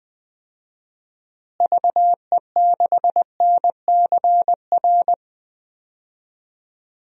Morse code, heard as a keyed tone, about 20 wpm, 700 Hz. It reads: VE6NCR
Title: DXCC-100 Call Signs - 6 Characters - Repeated 3x - 20wpm